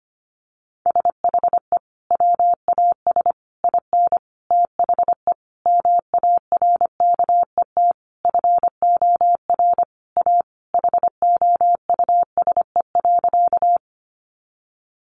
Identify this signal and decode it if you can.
Morse code, heard as a keyed tone, about 25 wpm, 700 Hz. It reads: SHE WAS IN THE MARKET FOR A HOUSE.
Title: Sentences from Top 600 Words 25wpm